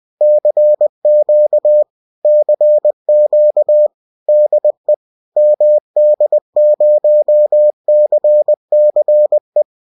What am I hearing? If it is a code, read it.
Morse code, heard as a keyed tone, about 20 wpm, 600 Hz. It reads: CQ CQ DE MD0CCE